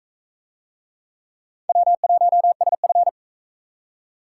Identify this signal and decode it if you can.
Morse code, heard as a keyed tone, about 36 wpm, 700 Hz. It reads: W1SF